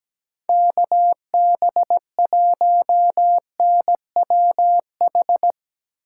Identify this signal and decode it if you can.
Morse code, heard as a keyed tone, about 17 wpm, 700 Hz. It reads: KB1NWH
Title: USA Call Signs - 6 Characters - 2x Character Spacing - 17wpm